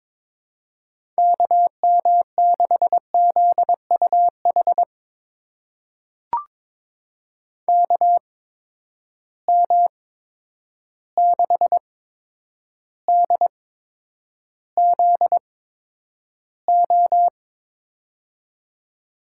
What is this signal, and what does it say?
Morse code, heard as a keyed tone, about 22 wpm, spaced as for 9 wpm, 700 Hz. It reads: KM6ZUH KM6DZO
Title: USA Call Signs - 6 Characters - 8x Character Spacing - 22wpm